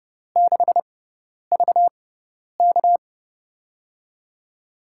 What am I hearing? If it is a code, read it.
Morse code, heard as a keyed tone, about 30 wpm, spaced as for 10 wpm, 700 Hz. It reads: 6VK
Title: USA Call Signs - 4 Characters - 6x Character Spacing - 30wpm